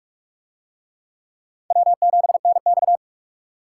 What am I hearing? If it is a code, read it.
Morse code, heard as a keyed tone, about 44 wpm, 700 Hz. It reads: W7NX